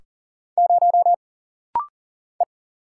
Morse code, heard as a keyed tone, about 40 wpm, 700 Hz. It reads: 0 E
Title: Single Letter-Number - Rapid-Fire 40wpm